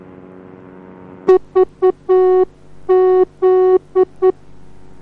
Morse code, heard as a keyed tone, about 9 wpm, 370 Hz. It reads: VZ